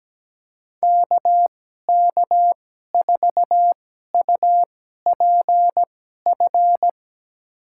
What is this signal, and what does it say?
Morse code, heard as a keyed tone, about 17 wpm, spaced as for 12 wpm, 700 Hz. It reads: KK4UPF
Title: USA Call Signs - 6 Characters - 2x Character Spacing - 17wpm